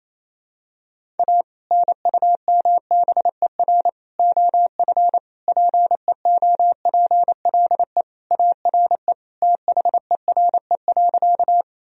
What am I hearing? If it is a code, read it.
Morse code, heard as a keyed tone, about 28 wpm, 700 Hz. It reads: A NUMBER OF PEOPLE ARE THERE.
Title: Sentences from Top 1000 Words - Review Part 1 - 28wpm